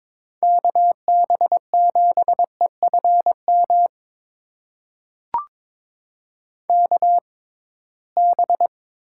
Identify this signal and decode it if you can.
Morse code, heard as a keyed tone, about 22 wpm, spaced as for 17 wpm, 700 Hz. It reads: KB7EFM KB
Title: USA Call Signs - 6 Characters - 6x Character Spacing - 22wpm